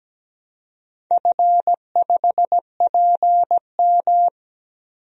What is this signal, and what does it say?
Morse code, heard as a keyed tone, about 17 wpm, 700 Hz. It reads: F5PM
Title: DXCC-100 Call Signs - 4 Characters - Repeated 3x - 17wpm